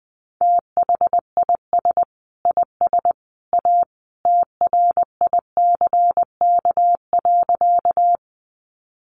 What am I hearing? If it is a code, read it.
Morse code, heard as a keyed tone, about 20 wpm, 700 Hz. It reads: THIS IS A TRICK.
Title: Sentences from Top 1500 Words 20wpm